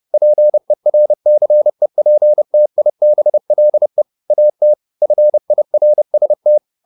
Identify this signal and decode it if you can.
Morse code, heard as a keyed tone, about 30 wpm, 600 Hz. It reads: PERCEPTIBLE AT FIRST